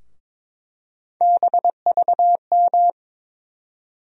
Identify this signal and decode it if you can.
Morse code, heard as a keyed tone, about 22 wpm, 700 Hz. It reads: BVM
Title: USA Call Sign Postfixes - 3 Letters 22wpm